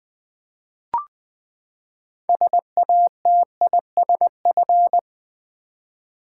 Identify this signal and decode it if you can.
Morse code, heard as a keyed tone, about 20 wpm, 700 Hz. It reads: SATISF
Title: Build A Word - 6 Letters At A Time - 12-Letter Words - 20wpm